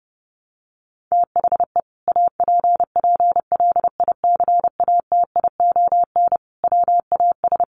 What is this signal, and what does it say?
Morse code, heard as a keyed tone, about 30 wpm, 700 Hz. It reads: THE APPLICATION WAS
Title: Sentences from Top 1200 Words 30wpm